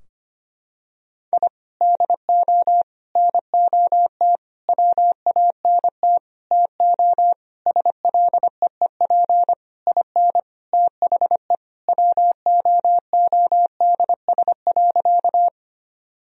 Morse code, heard as a keyed tone, about 25 wpm, 700 Hz. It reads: I DO NOT WANT TO SLEEP IN THE WOODS.
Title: Sentences from Top 1000 Words - Review Part 2 - 25wpm